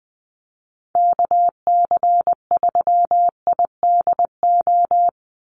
Morse code, heard as a keyed tone, about 20 wpm, 700 Hz. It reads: KC3IDO
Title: USA Call Signs 20wpm